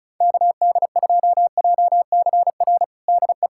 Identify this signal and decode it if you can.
Morse code, heard as a keyed tone, about 34 wpm, 700 Hz. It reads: KD2JCR DE
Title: USA Callsign DE Callsign 35wpm